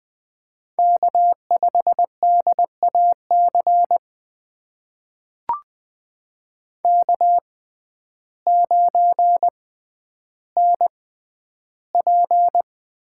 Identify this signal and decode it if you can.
Morse code, heard as a keyed tone, about 20 wpm, spaced as for 11 wpm, 700 Hz. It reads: K5DAC K9NP
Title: USA Call Signs - 5 Characters - 6x Character Spacing - 20wpm